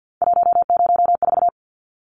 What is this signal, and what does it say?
Morse code, heard as a keyed tone, about 49 wpm, 700 Hz. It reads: 104
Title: Numbers - 3-Digit 50wpm